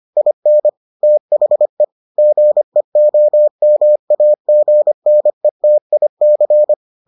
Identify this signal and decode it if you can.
Morse code, heard as a keyed tone, about 25 wpm, 600 Hz. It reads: IN THE GEOMAGNETIC